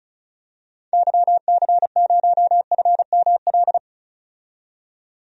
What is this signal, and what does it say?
Morse code, heard as a keyed tone, about 34 wpm, 700 Hz. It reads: YC0FML
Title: DXCC-100 Call Signs - 6 Characters - 35wpm